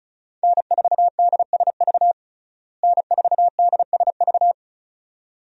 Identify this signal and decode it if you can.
Morse code, heard as a keyed tone, about 34 wpm, 700 Hz. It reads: N4DSV N4DSV
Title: USA Call Signs - 5 Characters - Repeated 2x - 35wpm